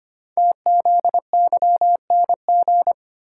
Morse code, heard as a keyed tone, about 25 wpm, 700 Hz. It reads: TZYNG
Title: Code Groups - 5 Letter:Numbers - 25wpm